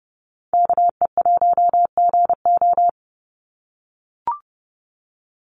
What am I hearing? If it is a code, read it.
Morse code, heard as a keyed tone, about 30 wpm, 700 Hz. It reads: KE1GO
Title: USA Call Signs 30wpm